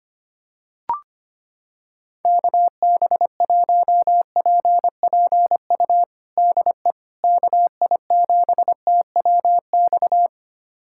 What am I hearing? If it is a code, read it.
Morse code, heard as a keyed tone, about 25 wpm, 700 Hz. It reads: KB1PPU DE KI7TWX
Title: USA Callsign DE Callsign - 6 Characters - 25wpm